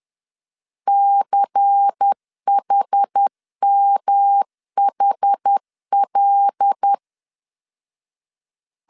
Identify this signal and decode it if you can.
Morse code, heard as a keyed tone, about 10 wpm, 800 Hz. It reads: CHMHL
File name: LEZIONI / 6 PKX0 / 6 P 10wpm 2432